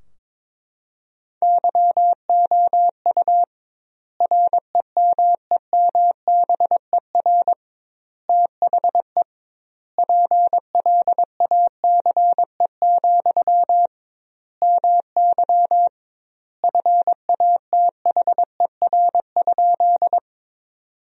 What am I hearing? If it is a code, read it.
Morse code, heard as a keyed tone, about 22 wpm, 700 Hz. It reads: YOU REMEMBER THE PLACE, MY FATHER?
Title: 2x Word-Spacing - Sentences from Top 400 Words - 22wpm